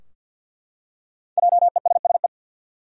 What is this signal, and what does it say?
Morse code, heard as a keyed tone, about 52 wpm, 700 Hz. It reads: JESSE
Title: USA Common Names - 50wpm